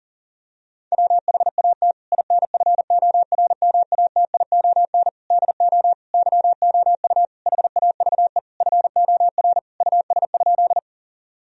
Morse code, heard as a keyed tone, about 40 wpm, 700 Hz. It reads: WHAT INFORMATION DO YOU HAVE FOR US?
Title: Sentences from Top 1000 Words - Review Part 3 - 40wpm